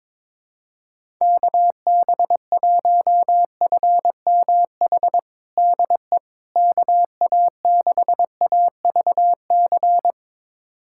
Morse code, heard as a keyed tone, about 22 wpm, 700 Hz. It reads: KB1FMH DE KA6AVC
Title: USA Callsign DE Callsign - 6 Characters - 22wpm